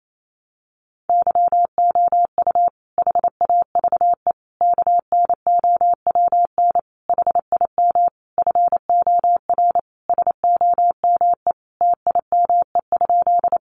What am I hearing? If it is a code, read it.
Morse code, heard as a keyed tone, about 28 wpm, 700 Hz. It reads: YOU HAVE KNOWN HIM FOR SOME TIME?
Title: Sentences from Top 600 Words 28wpm